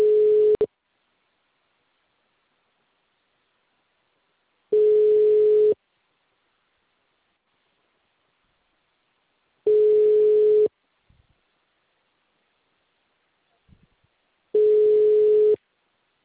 Ringback tone (Europe, 425 Hz).